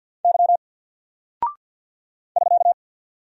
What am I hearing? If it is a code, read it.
Morse code, heard as a keyed tone, about 48 wpm, 700 Hz. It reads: Y <SK>
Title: Single Character (Letter:Number:Punctuation:Prosign) 50wpm